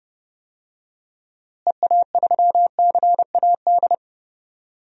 Morse code, heard as a keyed tone, about 30 wpm, 700 Hz. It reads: EA3CAD
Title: DXCC-100 Call Signs - 6 Characters - Repeated 3x - 30wpm